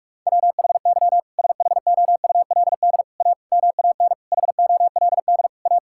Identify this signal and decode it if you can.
Morse code, heard as a keyed tone, about 46 wpm, 700 Hz. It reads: WHY SHOULD A MAN HOLD A
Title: Sentences from Top 1600 Words 45wpm